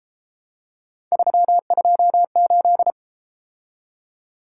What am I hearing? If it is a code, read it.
Morse code, heard as a keyed tone, about 31 wpm, 700 Hz. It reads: 328